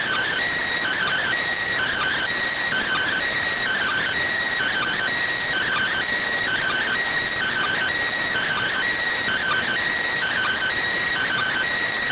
Two-tone teleprinter signal, about 400 Hz shift.